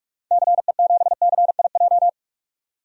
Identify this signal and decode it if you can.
Morse code, heard as a keyed tone, about 46 wpm, 700 Hz. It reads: CE7CIJ
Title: DXCC-100 Call Signs - 6 Characters - 45wpm